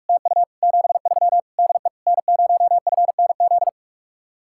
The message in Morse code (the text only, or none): TU 73 DE N0FNZ